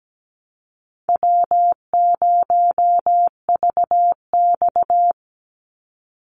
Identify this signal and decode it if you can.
Morse code, heard as a keyed tone, about 17 wpm, 700 Hz. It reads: W0VX
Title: Easy Call Signs - 17wpm